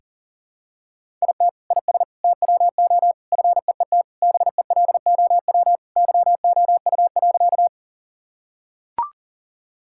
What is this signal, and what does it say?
Morse code, heard as a keyed tone, about 40 wpm, 700 Hz. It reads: IT IS TWO FEET BELOW YOU.
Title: Sentences from Top 1000 Words - Review Part 5 - 40wpm